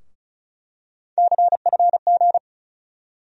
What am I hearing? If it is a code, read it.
Morse code, heard as a keyed tone, about 36 wpm, 700 Hz. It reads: CFG
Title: USA Call Sign Postfixes - 3 Letters 35wpm